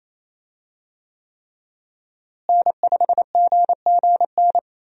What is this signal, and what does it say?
Morse code, heard as a keyed tone, about 28 wpm, 700 Hz. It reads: N5GGN
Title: USA Call Signs - 5 Characters - Repeated 3x - 28wpm